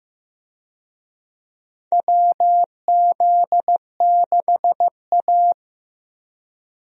Morse code, heard as a keyed tone, about 16 wpm, 700 Hz. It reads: WZ6A
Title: USA Call Signs - 4 Characters - Repeated 2x - 15wpm